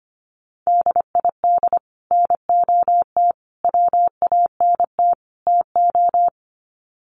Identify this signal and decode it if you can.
Morse code, heard as a keyed tone, about 25 wpm, 700 Hz. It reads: DID NOT WANT TO
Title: Sets of 4 Words 25wpm